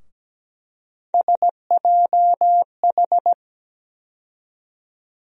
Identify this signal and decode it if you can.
Morse code, heard as a keyed tone, about 17 wpm, 700 Hz. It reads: SJH
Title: USA Call Sign Postfixes - 3 Letters 17wpm